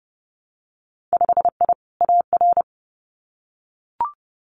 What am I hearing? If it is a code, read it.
Morse code, heard as a keyed tone, about 31 wpm, 700 Hz. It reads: 5I AR
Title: Winter Field Day Exchanges Encore 30wpm